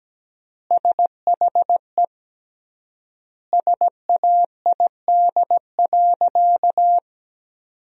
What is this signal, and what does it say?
Morse code, heard as a keyed tone, about 17 wpm, 700 Hz. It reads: SHE SAID.
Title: 3x Word-Spacing - Sentences from Top 400 Words - 17wpm